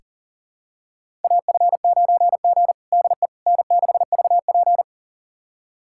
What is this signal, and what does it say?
Morse code, heard as a keyed tone, about 37 wpm, 700 Hz. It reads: AF9G DE N6VP
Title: USA Callsign DE Callsign - 4 Characters - 40wpm